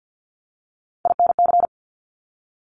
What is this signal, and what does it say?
Morse code, heard as a keyed tone, about 50 wpm, 690 Hz. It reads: INC
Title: Top Three Letter Combinations in Words 50wpm